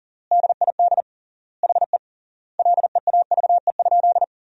Morse code, heard as a keyed tone, about 40 wpm, 700 Hz. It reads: DID HE LEAVE?
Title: 3x Word-Spacing - Sentences from Top 400 Words - 40wpm